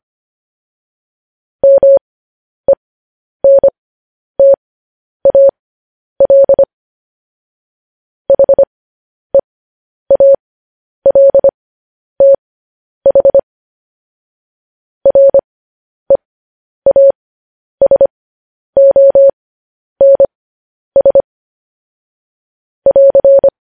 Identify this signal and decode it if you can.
Morse code, heard as a keyed tone, about 25 wpm, spaced as for 10 wpm, 550 Hz. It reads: MENTAL HEALTH REASONS <AR>